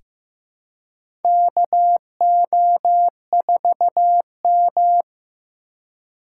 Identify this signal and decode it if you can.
Morse code, heard as a keyed tone, about 15 wpm, 700 Hz. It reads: KO4M
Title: USA Call Signs - 4 Characters - 4x Character Spacing - 15wpm